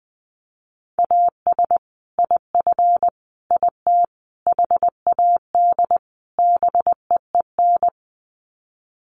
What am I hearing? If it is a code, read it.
Morse code, heard as a keyed tone, about 20 wpm, 700 Hz. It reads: AS IF IT HAD BEEN
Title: Sets of 5 Words Encore2 20wpm